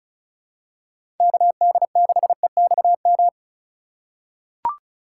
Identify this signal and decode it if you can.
Morse code, heard as a keyed tone, about 36 wpm, 700 Hz. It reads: KD6EXM
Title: USA Call Signs - 6 Characters - 2x Character Spacing - 35wpm